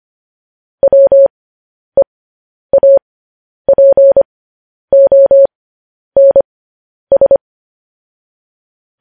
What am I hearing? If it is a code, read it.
Morse code, heard as a keyed tone, about 25 wpm, spaced as for 10 wpm, 550 Hz. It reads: WEAPONS
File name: podcast / News headlines-1x-25x10